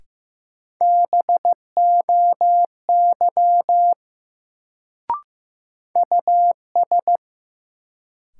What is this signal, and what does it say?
Morse code, heard as a keyed tone, about 15 wpm, 700 Hz. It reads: BOY US
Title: Sentences from Top 200 Words - New Word Drill - 15wpm